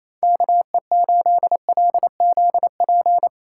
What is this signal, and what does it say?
Morse code, heard as a keyed tone, about 28 wpm, 700 Hz. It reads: KE8LZP
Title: USA Call Signs - 6 Characters - 3x Character Spacing - 28wpm